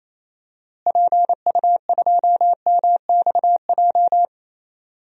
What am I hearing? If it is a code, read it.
Morse code, heard as a keyed tone, about 27 wpm, 700 Hz. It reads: PU2MXJ
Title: DXCC-100 Call Signs - 6 Characters - Repeated 2x - 28wpm